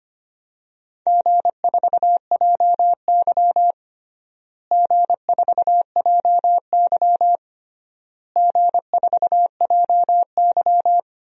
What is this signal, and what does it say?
Morse code, heard as a keyed tone, about 25 wpm, 700 Hz. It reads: G4JY G4JY G4JY
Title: DXCC-100 Call Signs - 4 Characters - Repeated 3x - 25wpm